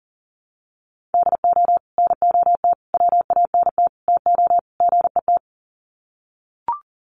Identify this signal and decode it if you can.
Morse code, heard as a keyed tone, about 40 wpm, 700 Hz. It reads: DO NOT WANT TO GET
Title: Sets of 5 Words 40wpm